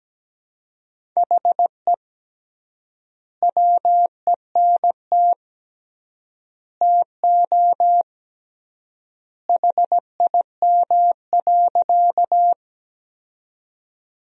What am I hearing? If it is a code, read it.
Morse code, heard as a keyed tone, about 17 wpm, 700 Hz. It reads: HE WENT TO HIM.
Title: 3x Word-Spacing - Sentences from Top 200 Words - 17wpm